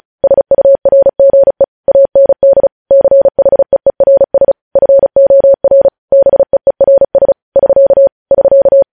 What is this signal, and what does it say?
Morse code, heard as a keyed tone, about 36 wpm, 550 Hz. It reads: SURGE AND CHEERS FOR BEERS <SK> <SK>
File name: podcast / News headlines-1x-35x35